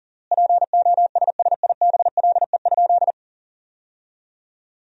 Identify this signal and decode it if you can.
Morse code, heard as a keyed tone, about 40 wpm, 700 Hz. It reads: POSSIBLE?
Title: 3x Word-Spacing - Sentences from Top 500 Words - 40wpm